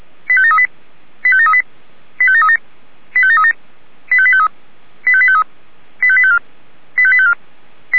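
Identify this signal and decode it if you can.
Five-tone selective call (ZVEI), heard as quick runs of five tones, tones of about 70 ms. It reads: 85738 85738 85737 85737 85853 85853 85854 85854
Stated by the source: ZVEI five-tone selective call received via SDR.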